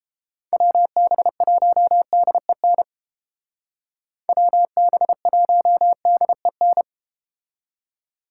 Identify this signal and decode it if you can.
Morse code, heard as a keyed tone, about 30 wpm, 700 Hz. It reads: WB1DEN WB1DEN